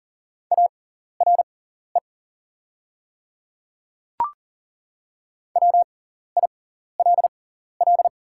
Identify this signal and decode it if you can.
Morse code, heard as a keyed tone, about 40 wpm, spaced as for 14 wpm, 700 Hz. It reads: ARE WILL
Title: 6x Character Spacing - Top 300 Words - 40wpm